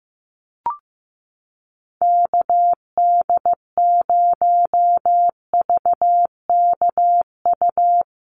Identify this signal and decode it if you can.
Morse code, heard as a keyed tone, about 15 wpm, 700 Hz. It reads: KD0VKU
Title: USA Call Signs Repeated 2x 15wpm